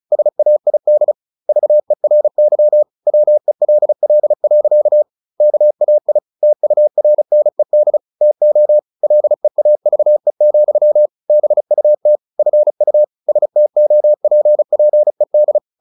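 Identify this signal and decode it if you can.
Morse code, heard as a keyed tone, about 34 wpm, 600 Hz. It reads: SAID VERY WELL. KAI TURNED TO LEAVE, BUT FU STOPPED
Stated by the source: Morse practice at 35 wpm from CW med Gnister, sending Ken Follett's novel Never in English.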